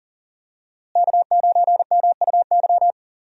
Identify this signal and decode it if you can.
Morse code, heard as a keyed tone, about 40 wpm, 700 Hz. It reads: K9MUY
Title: USA Call Signs - 5 Characters - Repeated 3x - 40wpm